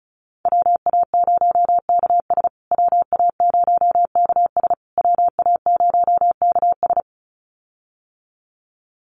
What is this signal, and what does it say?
Morse code, heard as a keyed tone, about 34 wpm, 700 Hz. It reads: WA0KS WA0KS WA0KS
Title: USA Call Signs Repeated 3x 35wpm